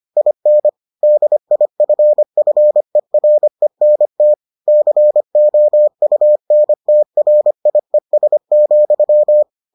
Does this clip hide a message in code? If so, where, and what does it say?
Yes: Morse code from 0.2 to 9.4 s, heard as a keyed tone, about 25 wpm, 600 Hz. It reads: IN DIFFERENT COUNTRIES,